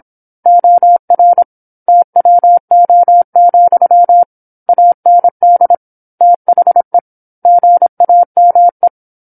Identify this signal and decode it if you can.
Morse code, heard as a keyed tone, about 27 wpm, 700 Hz. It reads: OR TWO, AND THE GAME
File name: 08-Chapter-08 26 WPM